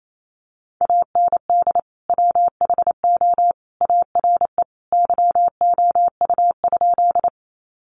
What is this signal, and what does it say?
Morse code, heard as a keyed tone, about 28 wpm, 700 Hz. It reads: AND WHO ARE YOU?
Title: Sentences from Top 100 Words 28wpm